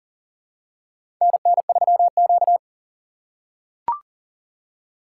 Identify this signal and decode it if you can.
Morse code, heard as a keyed tone, about 40 wpm, 700 Hz. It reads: NN3Q
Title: USA Call Signs - 4 Characters - 3x Character Spacing - 40wpm